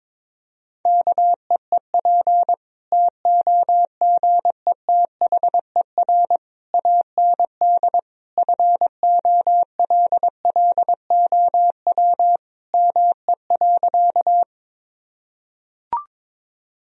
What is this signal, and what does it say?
Morse code, heard as a keyed tone, about 22 wpm, 700 Hz. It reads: KEEP TOGETHER AND FOLLOW ME.
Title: Sentences from Top 1000 Words - Review Part 1 - 22wpm